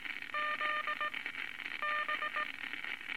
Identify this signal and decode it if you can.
Morse code, heard as a keyed tone, about 18 wpm, spaced as for 9 wpm, 1210 Hz, with noise. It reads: ZB
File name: Morse Code Records / 19 A LCM 166 Ten Words Per Minute